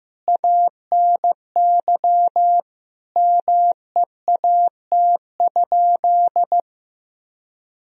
Morse code, heard as a keyed tone, about 15 wpm, 700 Hz. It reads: ANY MEAT?
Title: Sentences from Top 1000 Words - Review Part 3 - 15wpm